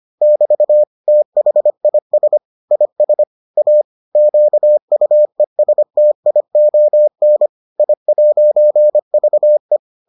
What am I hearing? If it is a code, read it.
Morse code, heard as a keyed tone, about 25 wpm, 600 Hz. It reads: <BT> THIS IS A QUESTION I'VE